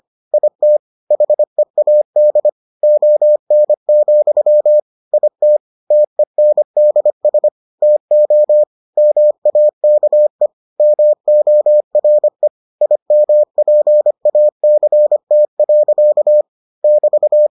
Morse code, heard as a keyed tone, about 25 wpm, 600 Hz. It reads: IT HEAD ON, IT TENDS TO MAKE MORE IMPACT. <BT>